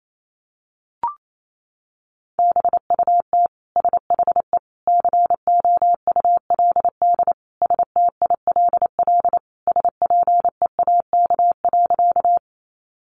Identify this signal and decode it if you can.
Morse code, heard as a keyed tone, about 28 wpm, 700 Hz. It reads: BUT SHE COULD STILL SPEAK.
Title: Sentences from Top 1100 Words 28wpm